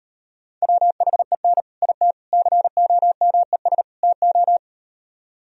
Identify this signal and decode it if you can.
Morse code, heard as a keyed tone, about 38 wpm, 700 Hz. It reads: WHEN IT COMES TO